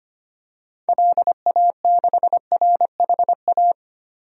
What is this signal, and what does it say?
Morse code, heard as a keyed tone, about 25 wpm, 700 Hz. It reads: LA6RHA
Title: DXCC-100 Call Signs - 6 Characters - 25wpm